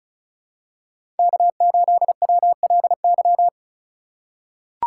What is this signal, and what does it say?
Morse code, heard as a keyed tone, about 36 wpm, 700 Hz. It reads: K8WLY
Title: USA Call Signs - 5 Characters - 3x Character Spacing - 35wpm